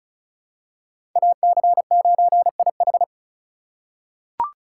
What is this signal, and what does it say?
Morse code, heard as a keyed tone, about 36 wpm, 700 Hz. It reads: AC9IH
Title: USA Call Signs - 4x Character Spacing - 35wpm